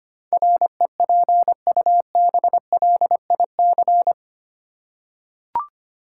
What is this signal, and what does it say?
Morse code, heard as a keyed tone, about 25 wpm, 700 Hz. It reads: REPUBLIC
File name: Eight Letter Words - 25wpm